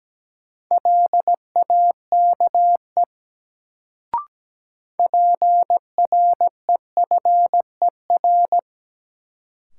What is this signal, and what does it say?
Morse code, heard as a keyed tone, about 17 wpm, 700 Hz. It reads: LAKE PREFER
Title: Sentences from Top 1100 Words - New Word Drill Alt - 17wpm